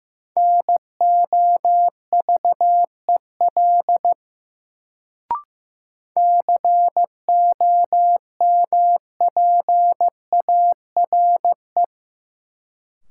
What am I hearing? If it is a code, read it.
Morse code, heard as a keyed tone, about 15 wpm, 700 Hz. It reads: NOVEL COMPARE
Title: Sentences from Top 1300 Words - New Word Drill - 15wpm